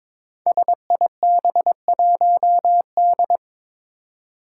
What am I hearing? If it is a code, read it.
Morse code, heard as a keyed tone, about 21 wpm, 700 Hz. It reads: SIB1D
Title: Five Challenging Letter-Number Combinations 22wpm